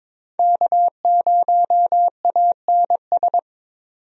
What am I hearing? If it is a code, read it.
Morse code, heard as a keyed tone, about 22 wpm, 700 Hz. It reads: K0ANS
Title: USA Call Signs - 5 Characters - 22wpm